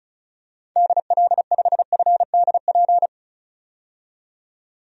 Morse code, heard as a keyed tone, about 36 wpm, 700 Hz. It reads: DL5FDP
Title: DXCC-100 Call Signs - 6 Characters - 35wpm